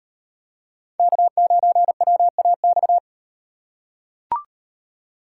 Morse code, heard as a keyed tone, about 35 wpm, 700 Hz. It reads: K9WAX